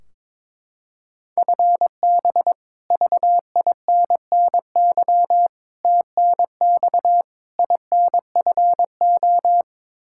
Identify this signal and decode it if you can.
Morse code, heard as a keyed tone, about 22 wpm, 700 Hz. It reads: FB VINNY TNX INFO
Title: Practice QSO - Exchange 3 - More Info - 22wpm